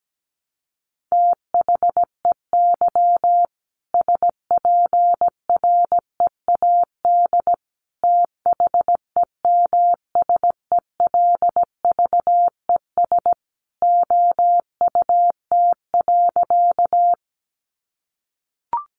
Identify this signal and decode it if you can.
Morse code, heard as a keyed tone, about 17 wpm, 700 Hz. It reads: THEY SPREAD THEMSELVES OUT.
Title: Sentences from Top 1200 Words 17wpm